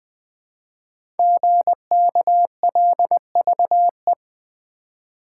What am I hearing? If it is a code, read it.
Morse code, heard as a keyed tone, about 20 wpm, 700 Hz. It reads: GKLVE
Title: Code Groups - 5 Letter:Numbers - 20wpm